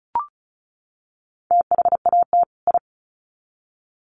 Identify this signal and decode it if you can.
Morse code, heard as a keyed tone, about 34 wpm, 700 Hz. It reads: THAT I
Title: Sets of 2 Words 35wpm